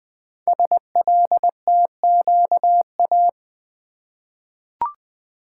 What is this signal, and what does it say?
Morse code, heard as a keyed tone, about 20 wpm, 700 Hz. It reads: SLTQA
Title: Code Groups - 5 Letters - 20wpm